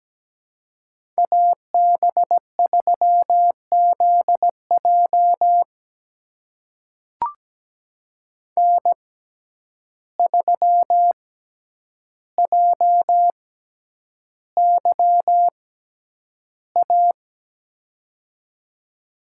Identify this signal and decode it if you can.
Morse code, heard as a keyed tone, about 17 wpm, spaced as for 9 wpm, 700 Hz. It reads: AB3ZJ N3JYA
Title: USA Call Signs - 5 Characters - 6x Character Spacing - 17wpm